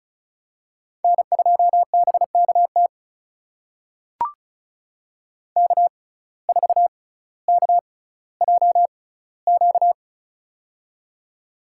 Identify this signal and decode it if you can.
Morse code, heard as a keyed tone, about 36 wpm, spaced as for 18 wpm, 700 Hz. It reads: N2BKT K4KJQ
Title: USA Call Signs - 5 Characters - 6x Character Spacing - 35wpm